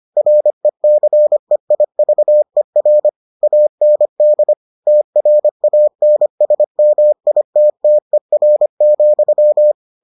Morse code, heard as a keyed tone, about 25 wpm, 600 Hz. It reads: RECEIVER AND TRANSMITTER,